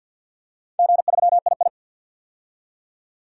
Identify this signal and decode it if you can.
Morse code, heard as a keyed tone, about 50 wpm, 700 Hz. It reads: G3II
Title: DXCC-100 Call Signs - 4 Characters - 50wpm